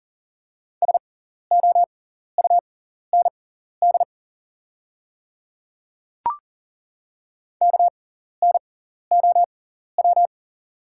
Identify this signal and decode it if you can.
Morse code, heard as a keyed tone, about 40 wpm, spaced as for 14 wpm, 700 Hz. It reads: SOUND KNOW
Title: 6x Character Spacing - Top 100 Words - 40wpm